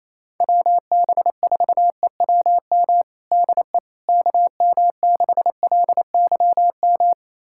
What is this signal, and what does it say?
Morse code, heard as a keyed tone, about 28 wpm, 700 Hz. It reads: WB4EWM DE KM6LYM
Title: USA Callsign DE Callsign 28wpm